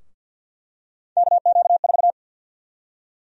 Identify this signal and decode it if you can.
Morse code, heard as a keyed tone, about 50 wpm, 700 Hz. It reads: KQ4
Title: USA Call Sign Prefixes - 50wpm